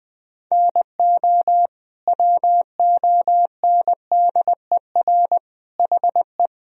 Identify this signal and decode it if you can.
Morse code, heard as a keyed tone, about 21 wpm, 700 Hz. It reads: NO WONDER HE
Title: Sentences from Top 1000 Words - Review Part 3 - 20wpm